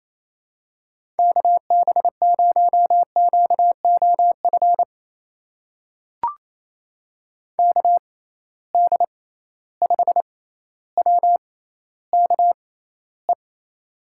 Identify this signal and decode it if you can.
Morse code, heard as a keyed tone, about 28 wpm, spaced as for 14 wpm, 700 Hz. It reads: KB0QOF KD5WKE
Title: USA Call Signs - 6 Characters - 6x Character Spacing - 28wpm